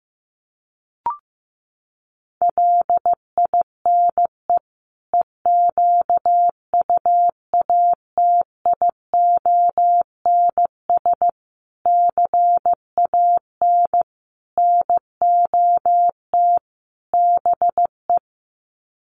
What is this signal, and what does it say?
Morse code, heard as a keyed tone, about 15 wpm, 700 Hz. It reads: LINE EQUATIONS CAN NOT BE
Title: Sets of 5 Words 15wpm